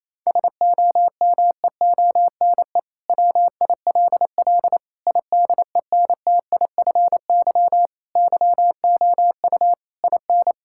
Morse code, heard as a keyed tone, about 28 wpm, 700 Hz. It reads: SOMEONE WILL IDENTIFY YOU IN
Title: Sentences from Top 1000 Words - Review Part 6 - 28wpm